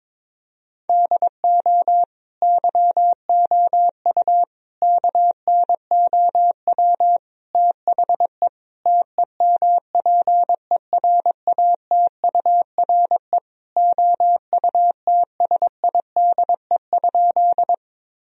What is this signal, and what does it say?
Morse code, heard as a keyed tone, about 22 wpm, 700 Hz. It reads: DO YOU KNOW THE TEMPERATURE OUTSIDE?
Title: Sentences from Top 1000 Words - Review Part 4 - 22wpm